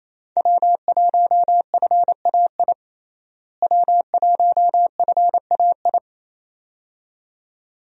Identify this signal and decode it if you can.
Morse code, heard as a keyed tone, about 28 wpm, 700 Hz. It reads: W1FAI W1FAI
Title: USA Call Signs - 5 Characters - Repeated 2x - 28wpm